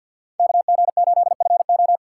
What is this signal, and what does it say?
Morse code, heard as a keyed tone, about 53 wpm, 700 Hz. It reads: KG8FO
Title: USA Call Signs - 4x Character Spacing - 50wpm